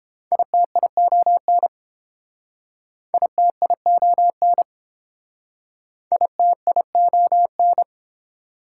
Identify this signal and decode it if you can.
Morse code, heard as a keyed tone, about 30 wpm, 700 Hz. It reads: ITION ITION ITION